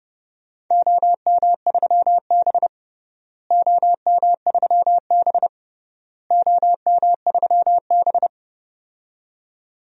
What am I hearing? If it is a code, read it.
Morse code, heard as a keyed tone, about 31 wpm, 700 Hz. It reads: OM3B OM3B OM3B
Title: DXCC-100 Call Signs - 4 Characters - Repeated 3x - 30wpm